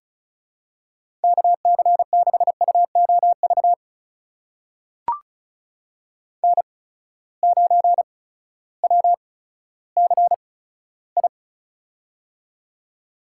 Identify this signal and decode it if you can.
Morse code, heard as a keyed tone, about 36 wpm, spaced as for 16 wpm, 700 Hz. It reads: KC6UOV N9WCI
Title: USA Call Signs - 8x Character Spacing - 35wpm